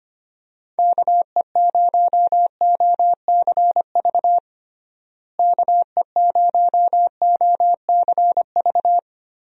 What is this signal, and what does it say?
Morse code, heard as a keyed tone, about 25 wpm, 700 Hz. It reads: KE0OCV KE0OCV
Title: USA Call Signs - 6 Characters - Repeated 2x - 25wpm